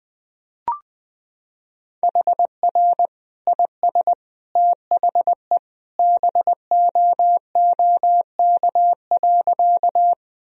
Morse code, heard as a keyed tone, about 20 wpm, 700 Hz. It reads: HR IS THE BOOK.
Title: Sentences from Top 300 Words and Abbreviations 20wpm